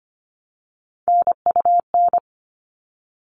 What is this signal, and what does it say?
Morse code, heard as a keyed tone, about 25 wpm, 700 Hz. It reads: NUN
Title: All Cut Numbers - 3-Digit 25wpm